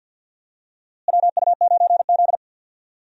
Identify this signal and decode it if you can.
Morse code, heard as a keyed tone, about 53 wpm, 700 Hz. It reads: WU9Z
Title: USA Call Signs - 4 Characters - Repeated 3x - 50wpm